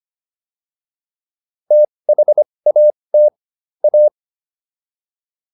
Morse code, heard as a keyed tone, about 25 wpm, spaced as for 20 wpm, 600 Hz. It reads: THAT A